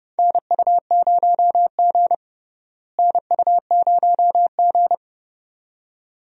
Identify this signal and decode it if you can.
Morse code, heard as a keyed tone, about 31 wpm, 700 Hz. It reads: NU0G NU0G
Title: USA Call Signs - 4 Characters - Repeated 2x - 30wpm